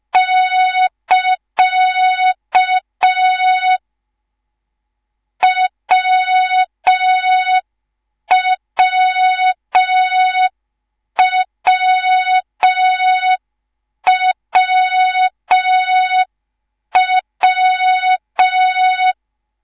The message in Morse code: <KA> WWWWW